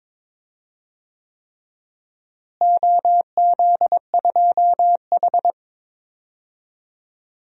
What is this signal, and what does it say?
Morse code, heard as a keyed tone, about 22 wpm, 700 Hz. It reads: OZ2H
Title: DXCC-100 Call Signs - 4 Characters - Repeated 3x - 22wpm